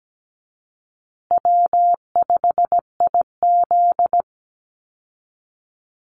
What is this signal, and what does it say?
Morse code, heard as a keyed tone, about 17 wpm, 700 Hz. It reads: W5IZ
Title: Easy Call Signs Repeated 3x - 17wpm